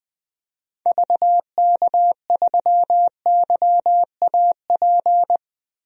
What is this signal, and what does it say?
Morse code, heard as a keyed tone, about 20 wpm, 700 Hz. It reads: VK3YAP
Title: DXCC-100 Call Signs - 6 Characters - 20wpm